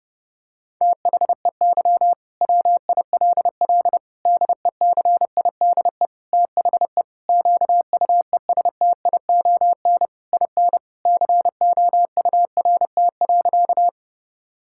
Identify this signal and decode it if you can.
Morse code, heard as a keyed tone, about 30 wpm, 700 Hz. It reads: THEY WILL DECIDE THE QUESTION IN COURT.
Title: Sentences from Top 1600 Words 30wpm